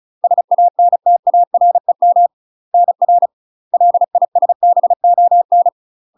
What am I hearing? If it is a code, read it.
Morse code, heard as a keyed tone, about 36 wpm, 700 Hz. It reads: SANTAREM NR LISBON
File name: UK MASTER QSOs TRAINING / 35WPM / UK MASTER QSOs 0018